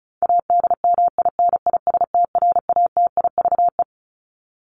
Morse code, heard as a keyed tone, about 34 wpm, 700 Hz. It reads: ADMINISTRATIVE
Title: Top 1501-1750 Words 35wpm